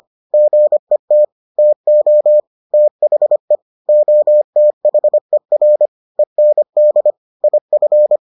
Morse code, heard as a keyed tone, about 25 wpm, 600 Hz. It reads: GET TO THE OTHER END IF